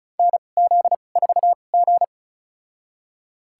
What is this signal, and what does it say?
Morse code, heard as a keyed tone, about 36 wpm, spaced as for 25 wpm, 700 Hz. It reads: NZ4G
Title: USA Call Signs - 2x Character Spacing - 35wpm